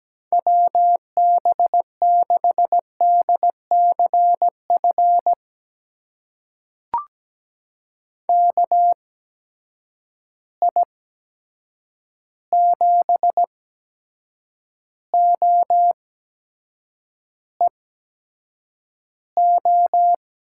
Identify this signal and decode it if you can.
Morse code, heard as a keyed tone, about 17 wpm, spaced as for 7 wpm, 700 Hz. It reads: WB6DCF KI7OEO
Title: USA Call Signs - 6 Characters - 8x Character Spacing - 17wpm